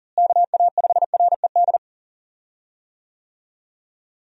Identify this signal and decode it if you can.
Morse code, heard as a keyed tone, about 37 wpm, 700 Hz. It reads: KA5RED
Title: USA Call Signs - 6 Characters - Repeated 3x - 40wpm